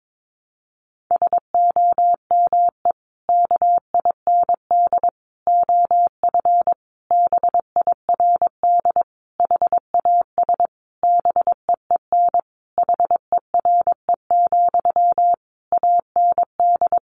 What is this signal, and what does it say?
Morse code, heard as a keyed tone, about 22 wpm, 700 Hz. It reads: SOME KIND OF BIRD HAS BEEN HERE, AND
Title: Sentences from Top 1200 Words 22wpm